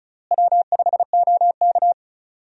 Morse code, heard as a keyed tone, about 34 wpm, 700 Hz. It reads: W5OK
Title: USA Call Signs - 4 Characters - Repeated 2x - 35wpm